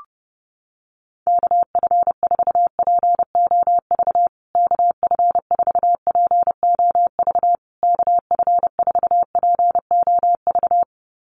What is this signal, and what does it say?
Morse code, heard as a keyed tone, about 30 wpm, 700 Hz. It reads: KF4POV KF4POV KF4POV
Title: USA Call Signs Repeated 3x 30wpm